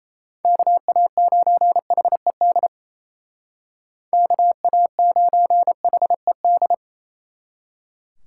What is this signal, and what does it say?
Morse code, heard as a keyed tone, about 33 wpm, 700 Hz. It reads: KA9HED KA9HED
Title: Speed Racing - USA 2x3 Call Signs 28wpm